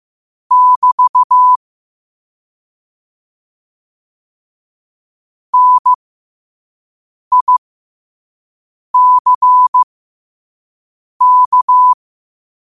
Morse code, heard as a keyed tone, about 15 wpm, spaced as for 5 wpm, 1000 Hz. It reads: <BT> NICK
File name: qotdfiles / 5wpm